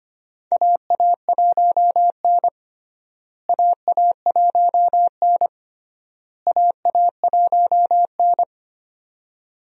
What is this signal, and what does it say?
Morse code, heard as a keyed tone, about 25 wpm, 700 Hz. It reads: AA1N AA1N AA1N
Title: USA Call Signs - 4 Characters - Repeated 3x - 25wpm